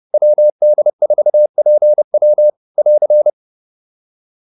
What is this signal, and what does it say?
Morse code, heard as a keyed tone, about 30 wpm, 600 Hz. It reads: WD4PW <AR>